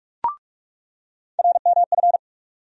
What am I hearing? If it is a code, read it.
Morse code, heard as a keyed tone, about 42 wpm, 700 Hz. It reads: RMF
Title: USA Call Sign Postfixes - 3 Letters 45wpm